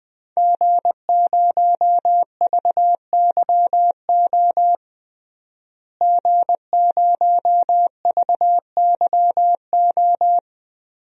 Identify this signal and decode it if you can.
Morse code, heard as a keyed tone, about 20 wpm, 700 Hz. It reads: G0VYO G0VYO
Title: DXCC-100 Call Signs - 5 Characters - Repeated 3x - 20wpm